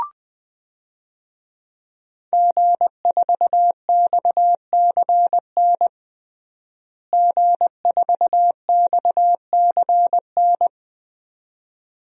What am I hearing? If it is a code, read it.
Morse code, heard as a keyed tone, about 20 wpm, 700 Hz. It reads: G4XCN G4XCN
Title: DXCC-100 Call Signs - 5 Characters - Repeated 2x - 20wpm